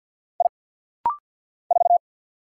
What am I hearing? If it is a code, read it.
Morse code, heard as a keyed tone, about 50 wpm, 700 Hz. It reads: I 4
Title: Single Letter-Number Alt - Rapid-Fire 50wpm